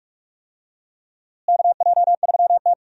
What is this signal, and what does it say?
Morse code, heard as a keyed tone, about 46 wpm, 700 Hz. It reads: KJ3T
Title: USA Call Signs - 4 Characters - Repeated 3x - 45wpm